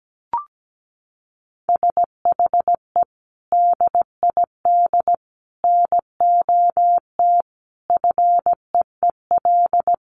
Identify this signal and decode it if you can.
Morse code, heard as a keyed tone, about 17 wpm, 700 Hz. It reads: SHE DID NOT FEEL
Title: Sentences from Top 1500 Words 17wpm